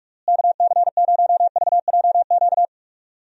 Morse code, heard as a keyed tone, about 42 wpm, 700 Hz. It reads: KC0VJQ
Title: USA Call Signs - 6 Characters - Repeated 3x - 45wpm